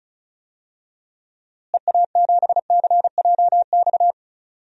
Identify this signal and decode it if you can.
Morse code, heard as a keyed tone, about 34 wpm, 700 Hz. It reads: EA7CJX
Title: DXCC-100 Call Signs - 6 Characters - Repeated 3x - 35wpm